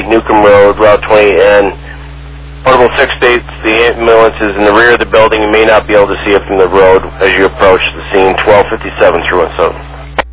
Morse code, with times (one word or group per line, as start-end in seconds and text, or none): none